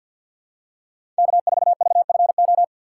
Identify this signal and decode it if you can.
Morse code, heard as a keyed tone, about 53 wpm, 700 Hz. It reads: K4VFO
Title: USA Call Signs - 4x Character Spacing - 50wpm